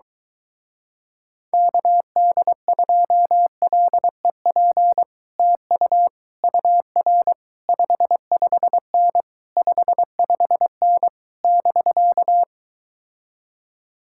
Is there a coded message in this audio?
Yes: Morse code from 1.5 to 12.4 s, heard as a keyed tone, about 23 wpm, 700 Hz. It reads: KD2LEP TU UR 55N 55N <BK>